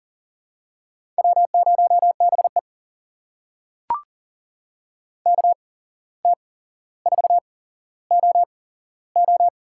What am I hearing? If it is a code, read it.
Morse code, heard as a keyed tone, about 38 wpm, spaced as for 16 wpm, 700 Hz. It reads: W0BE KT4OO
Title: USA Call Signs - 8x Character Spacing - 40wpm